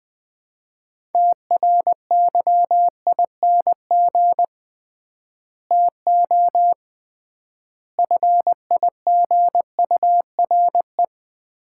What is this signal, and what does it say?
Morse code, heard as a keyed tone, about 21 wpm, 700 Hz. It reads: TRYING TO FIGURE
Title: 3x Word-Spacing - Sets of 5 Words - 20wpm